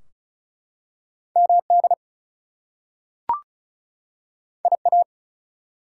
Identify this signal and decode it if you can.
Morse code, heard as a keyed tone, about 36 wpm, 700 Hz. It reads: MD IA
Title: US States and Canadian Provinces 35wpm